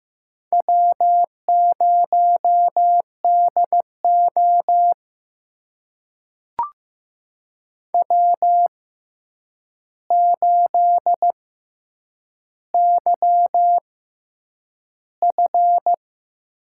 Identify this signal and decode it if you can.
Morse code, heard as a keyed tone, about 15 wpm, spaced as for 8 wpm, 700 Hz. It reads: W0DO W8YF
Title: USA Call Signs - 4 Characters - 6x Character Spacing - 15wpm